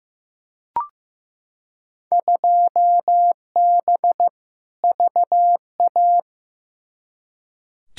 Morse code, heard as a keyed tone, about 15 wpm, 700 Hz. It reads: 2B VA
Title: Field Day Exchanges Encore - 15wpm